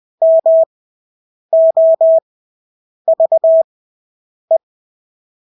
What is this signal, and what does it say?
Morse code, heard as a keyed tone, about 20 wpm, spaced as for 8 wpm, 650 Hz. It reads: MOVE
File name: Engelske ord 20/8 wpm